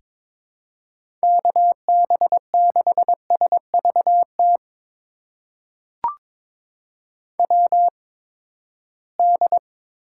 Morse code, heard as a keyed tone, about 22 wpm, spaced as for 15 wpm, 700 Hz. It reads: KB6SVT WD
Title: USA Call Signs - 6 Characters - 8x Character Spacing - 22wpm